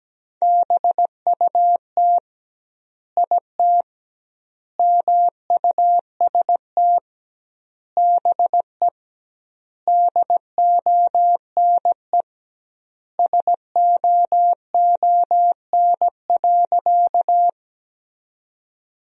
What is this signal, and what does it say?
Morse code, heard as a keyed tone, about 17 wpm, 700 Hz. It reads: BUT IT MUST BE DONE SOON.
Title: 2x Word-Spacing - Sentences from Top 500 Words - 17wpm